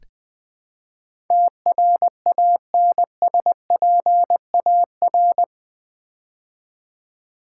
Morse code, heard as a keyed tone, about 20 wpm, 700 Hz. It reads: TRANSPAR
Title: Build A Word - 4 Letters At A Time - 12-Letter Words - 20wpm